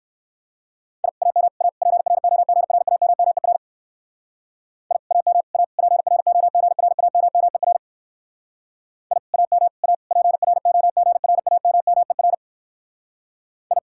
Morse code, heard as a keyed tone, about 56 wpm, 700 Hz. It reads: I AM A PROGRAMMER I AM A PROGRAMMER I AM A PROGRAMMER I